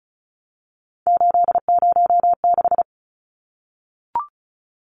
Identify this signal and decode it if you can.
Morse code, heard as a keyed tone, about 34 wpm, 700 Hz. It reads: 806
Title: Numbers - 3-Digit 35wpm